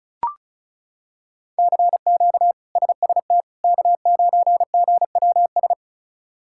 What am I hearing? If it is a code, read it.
Morse code, heard as a keyed tone, about 36 wpm, 700 Hz. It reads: CQ SST K9GWS
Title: K1USN SST QSO Practice - 35wpm